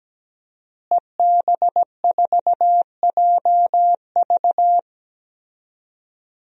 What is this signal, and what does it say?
Morse code, heard as a keyed tone, about 17 wpm, 700 Hz. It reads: EB4JV
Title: DXCC-100 Call Signs - 5 Characters - Repeated 3x - 17wpm